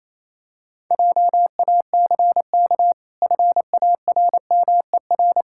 Morse code, heard as a keyed tone, about 27 wpm, 700 Hz. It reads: JACK FARMER
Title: Pangram Challenge v2 - 28wpm